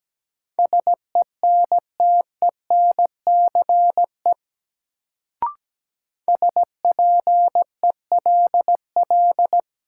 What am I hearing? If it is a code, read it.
Morse code, heard as a keyed tone, about 17 wpm, 700 Hz. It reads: SENTENCE SPELL
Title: Sentences from Top 1000 Words - Review Part 1 - New Word Drill - 17wpm